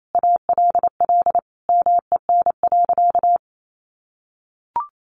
Morse code, heard as a keyed tone, about 28 wpm, 700 Hz. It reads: ALL MEN.
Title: Sentences from Top 1200 Words 28wpm